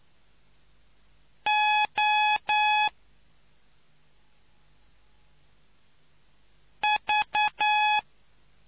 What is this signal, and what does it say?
Morse code, heard as a keyed tone, about 9 wpm, 840 Hz. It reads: O V